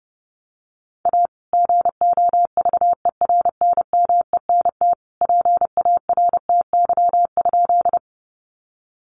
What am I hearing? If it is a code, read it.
Morse code, heard as a keyed tone, about 30 wpm, 700 Hz. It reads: A GOVERNMENT PARTY?
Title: Sentences from Top 1400 Words 30wpm